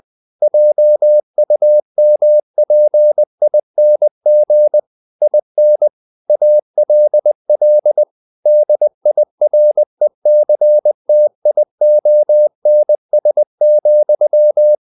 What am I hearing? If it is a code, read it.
Morse code, heard as a keyed tone, about 20 wpm, 600 Hz. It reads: JUMPING IN ALL DIRECTIONS,
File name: the old man and the sea / the old man and the sea part 1 20wpm